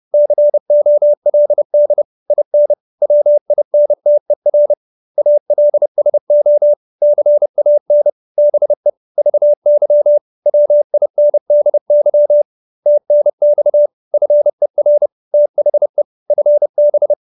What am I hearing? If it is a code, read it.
Morse code, heard as a keyed tone, about 30 wpm, 590 Hz. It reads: COLD IN WINTER ALSO CAN BE VY WINDY TNX FER THE FB